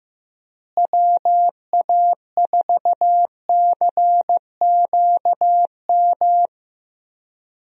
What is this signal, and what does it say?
Morse code, heard as a keyed tone, about 16 wpm, 700 Hz. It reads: WA4CQM
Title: USA Call Signs - 6 Characters - Repeated 2x - 15wpm